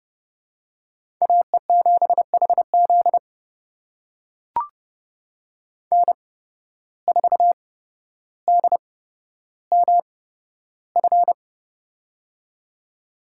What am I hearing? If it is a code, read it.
Morse code, heard as a keyed tone, about 30 wpm, spaced as for 13 wpm, 700 Hz. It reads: AE7HZ N4DMF
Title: USA Call Signs - 5 Characters - 8x Character Spacing - 30wpm